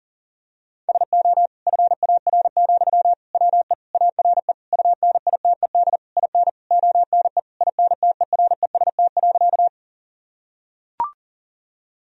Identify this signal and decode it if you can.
Morse code, heard as a keyed tone, about 40 wpm, 700 Hz. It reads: SO FAR, WE ARE UNITED IN ONE INTEREST.
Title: Sentences from Top 1000 Words - Review Part 4 - 40wpm